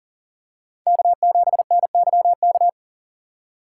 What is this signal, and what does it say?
Morse code, heard as a keyed tone, about 38 wpm, 700 Hz. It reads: K7NYK
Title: USA Call Signs - 2x Character Spacing - 40wpm